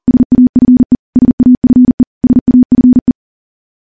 Morse code, heard as a keyed tone, about 40 wpm, 250 Hz. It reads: SAFE SAFE SAFE
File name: morse / morse-training-40WPM